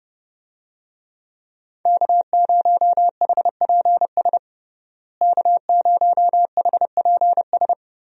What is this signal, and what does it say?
Morse code, heard as a keyed tone, about 30 wpm, 700 Hz. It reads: K0HPS K0HPS
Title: USA Call Signs - 5 Characters - Repeated 3x - 30wpm